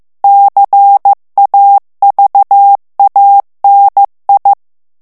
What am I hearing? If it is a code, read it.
Morse code, heard as a keyed tone, about 15 wpm, 800 Hz. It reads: CAVANI
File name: ITALIEN texte clair 1 vitesse 14 mots minute